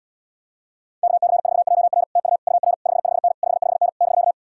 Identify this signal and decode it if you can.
Morse code, heard as a keyed tone, about 80 wpm, 700 Hz. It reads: KC4JR TU UR 55N 55N <BK>